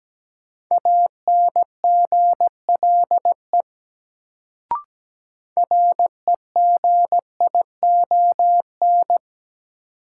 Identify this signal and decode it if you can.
Morse code, heard as a keyed tone, about 17 wpm, 700 Hz. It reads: ANGLE REGION
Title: Sentences from Top 1500 Words - New Word Drill - 17wpm